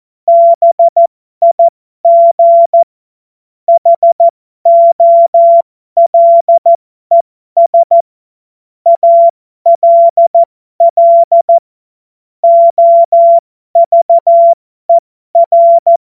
Morse code, heard as a keyed tone, about 14 wpm, spaced as for 12 wpm, 680 Hz. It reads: BIG HOLES ALL OVER
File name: Morsekurs / Witze